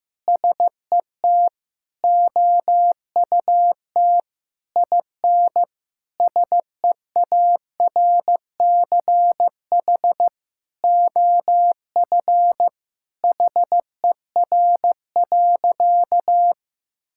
Morse code, heard as a keyed tone, about 15 wpm, 700 Hz. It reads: SET OUT IN SEARCH OF HER.
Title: Sentences from Top 1000 Words - Review Part 5 - 15wpm